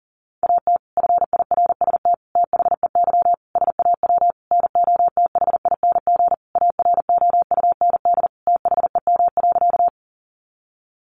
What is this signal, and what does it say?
Morse code, heard as a keyed tone, about 40 wpm, 700 Hz. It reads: AT FIRST THEY SAW NOTHING AROUND THEM.
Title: Sentences from Top 600 Words 40wpm